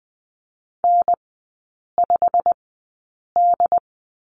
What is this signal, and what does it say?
Morse code, heard as a keyed tone, about 20 wpm, spaced as for 8 wpm, 700 Hz. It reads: N5D
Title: 152 Lesson 25e - Letter p - Three Characters 20wpm